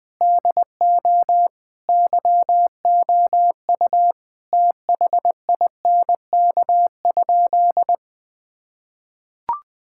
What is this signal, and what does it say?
Morse code, heard as a keyed tone, about 20 wpm, 700 Hz. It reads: DO YOU THINK?
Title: Sentences with 3-Word Phrases - Spoken Beginning - 20wpm